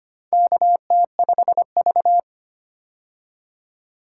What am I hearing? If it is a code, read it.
Morse code, heard as a keyed tone, about 25 wpm, 700 Hz. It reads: KT5V
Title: K1USN SST QSO Practice - 25wpm